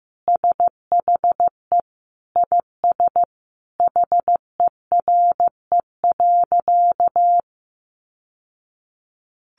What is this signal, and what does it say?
Morse code, heard as a keyed tone, about 14 wpm, 700 Hz. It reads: SHE IS HERE.
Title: Sentences from Top 600 Words 15wpm